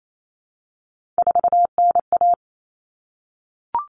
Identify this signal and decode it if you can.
Morse code, heard as a keyed tone, about 28 wpm, 700 Hz. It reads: 4NA
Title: Common Cut Numbers - 3-Digit 28wpm